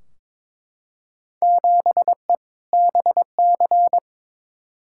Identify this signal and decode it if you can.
Morse code, heard as a keyed tone, about 22 wpm, 700 Hz. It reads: 7E BC
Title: Field Day Exchanges Encore - 22wpm